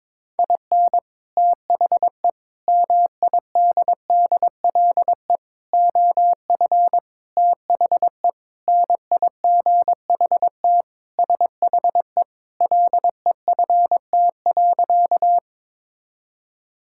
Morse code, heard as a keyed tone, about 22 wpm, 700 Hz. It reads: IN THE MIDDLE OF THE NIGHT SHE LEFT.
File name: practice / 22wpm / Sentences with 4-Word Phrases - 22wpm